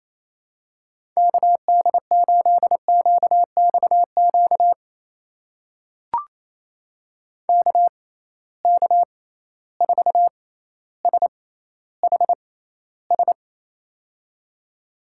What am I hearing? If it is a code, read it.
Morse code, heard as a keyed tone, about 28 wpm, spaced as for 14 wpm, 700 Hz. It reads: KD8QXQ KK4SHS
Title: USA Call Signs - 6 Characters - 6x Character Spacing - 28wpm